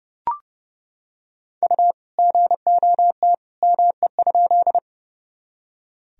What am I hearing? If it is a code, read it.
Morse code, heard as a keyed tone, about 30 wpm, 700 Hz. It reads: U GOT ME?
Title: Sentences from Top 300 Words and Abbreviations 30wpm